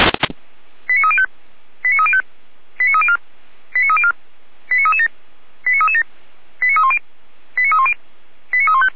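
Five-tone selective call (ZVEI), heard as quick runs of five tones, tones of about 70 ms. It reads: 89305 89305 89304 89304 89337 89337 89310 89310 89315
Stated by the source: ZVEI five-tone call received by SDR, logged by Feuerwehr Orlamünde (fire brigade).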